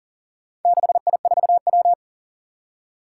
Morse code, heard as a keyed tone, about 42 wpm, 700 Hz. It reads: 6I4W
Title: Four Challenging Letter-Number Combinations 40wpm